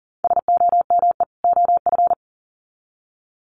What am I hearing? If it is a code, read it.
Morse code, heard as a keyed tone, about 37 wpm, 700 Hz. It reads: SOME OF
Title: Sets of 2 Words 40wpm